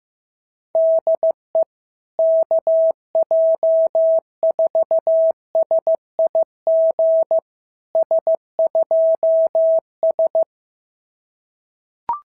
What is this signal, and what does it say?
Morse code, heard as a keyed tone, about 15 wpm, 650 Hz. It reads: DE KJ4SIG S2S
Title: SOTA QSO Practice 15wpm